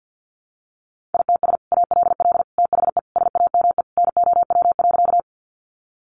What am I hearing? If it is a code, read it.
Morse code, heard as a keyed tone, about 50 wpm, 700 Hz. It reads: ITS ALL THE SAME NOW.
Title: Sentences from Top 600 Words 50wpm